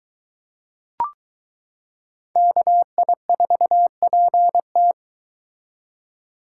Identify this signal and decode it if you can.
Morse code, heard as a keyed tone, about 22 wpm, 700 Hz. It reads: KI4PT